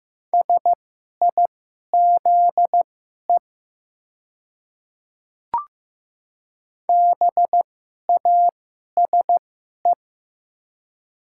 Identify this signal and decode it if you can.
Morse code, heard as a keyed tone, about 16 wpm, spaced as for 11 wpm, 700 Hz. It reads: SIZE BASE
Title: 2x Character Spacing - Top 500 Words - 15wpm